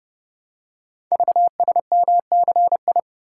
Morse code, heard as a keyed tone, about 28 wpm, 700 Hz. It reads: VSMCI
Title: Code Groups - 5 Letter:Numbers - 30wpm